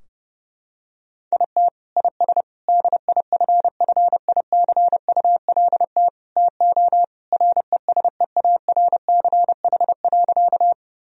Morse code, heard as a keyed tone, about 32 wpm, 700 Hz. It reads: IT IS DIFFICULT TO RESEARCH.
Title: Sentences from Top 1000 Words - Review Part 4 - 30wpm